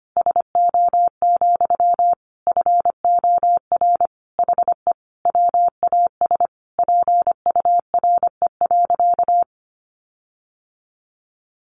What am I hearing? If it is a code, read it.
Morse code, heard as a keyed tone, about 25 wpm, 700 Hz. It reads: SO, FOR HE WAS PURE.
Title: Sentences from Top 1500 Words 25wpm